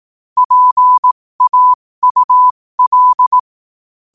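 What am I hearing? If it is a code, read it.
Morse code, heard as a keyed tone, about 18 wpm, spaced as for 15 wpm, 1000 Hz. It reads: PAUL
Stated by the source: Morse code quote of the day sent at 15 words per minute.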